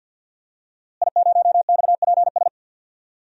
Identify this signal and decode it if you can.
Morse code, heard as a keyed tone, about 53 wpm, 700 Hz. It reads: I0XPS
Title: DXCC-100 Call Signs - 5 Characters - Repeated 2x - 50wpm